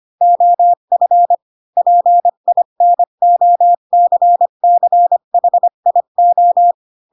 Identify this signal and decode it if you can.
Morse code, heard as a keyed tone, about 25 wpm, 700 Hz. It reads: OF PINOCCHIO
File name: Pinocchio / Pinocchio ch4 20 25 ik1hge